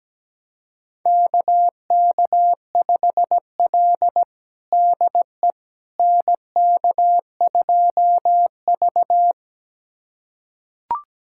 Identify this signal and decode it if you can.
Morse code, heard as a keyed tone, about 17 wpm, 700 Hz. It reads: KK5L DE NK2V
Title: USA Callsign DE Callsign - 4 Characters - 17wpm